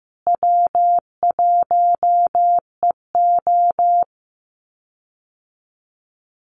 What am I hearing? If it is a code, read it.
Morse code, heard as a keyed tone, about 15 wpm, 700 Hz. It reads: W1EO
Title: USA Call Signs Repeated 3x 15wpm